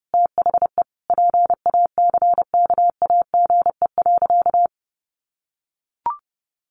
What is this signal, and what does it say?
Morse code, heard as a keyed tone, about 30 wpm, 700 Hz. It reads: THE PACKAGE.
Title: Additional Easy Phrases 30wpm